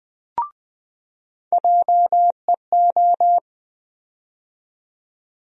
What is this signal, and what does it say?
Morse code, heard as a keyed tone, about 21 wpm, 700 Hz. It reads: JEO
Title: USA Call Sign Postfixes - 3 Letters 20wpm